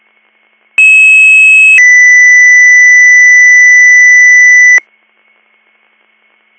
Two-tone page, heard as a one-second tone, then a three-second tone, about 2600 then 2000 Hz. It is heard from 0.8 to 4.8 s.